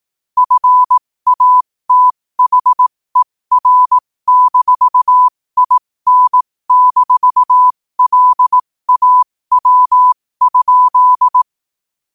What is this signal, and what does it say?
Morse code, heard as a keyed tone, about 18 wpm, spaced as for 15 wpm, 1000 Hz. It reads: FATHER-IN-LAW?
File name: qotdfiles / 15wpm